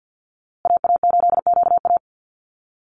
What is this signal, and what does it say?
Morse code, heard as a keyed tone, about 50 wpm, 700 Hz. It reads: AA8QA
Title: USA Call Signs Repeated 3x 50wpm